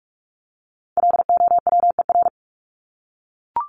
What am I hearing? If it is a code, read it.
Morse code, heard as a keyed tone, about 44 wpm, 700 Hz. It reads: LOWER
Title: Top 501-750 Words 45wpm